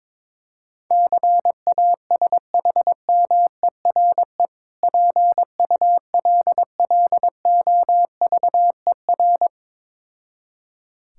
Morse code, heard as a keyed tone, about 22 wpm, 690 Hz. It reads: CASHMERE PULLOVER
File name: Difficult Words with Hints - 22wpm